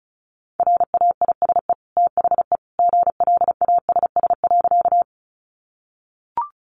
Morse code, heard as a keyed tone, about 34 wpm, 700 Hz. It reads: RAISE THE GLASS.
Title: Sentences from Top 1400 Words 35wpm